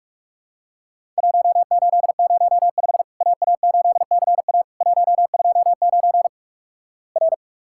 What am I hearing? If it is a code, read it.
Morse code, heard as a keyed tone, about 48 wpm, 700 Hz. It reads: 1805 AA8CA 129 R
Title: Worked All Europe DX Contest - QTC Exchange - 45wpm